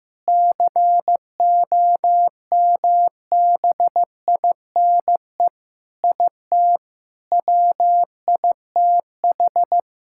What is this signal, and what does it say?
Morse code, heard as a keyed tone, about 15 wpm, 700 Hz. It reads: COMBINE IT WITH
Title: Sentences from Top 1600 Words 15wpm